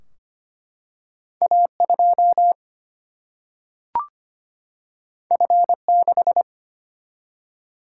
Morse code, heard as a keyed tone, about 25 wpm, 700 Hz. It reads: A2 F6
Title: Single-Letter Single-Number 25wpm